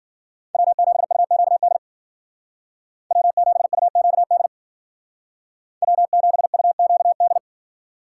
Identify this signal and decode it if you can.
Morse code, heard as a keyed tone, about 51 wpm, 700 Hz. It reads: W7UQD W7UQD W7UQD